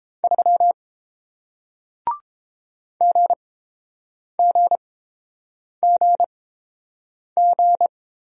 Morse code, heard as a keyed tone, about 30 wpm, spaced as for 7 wpm, 700 Hz. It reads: 3 GGGG